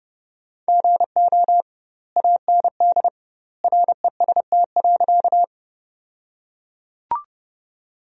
Morse code, heard as a keyed tone, about 32 wpm, 700 Hz. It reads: GO AND REST.
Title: 2x Word-Spacing - Sentences from Top 500 Words - 30wpm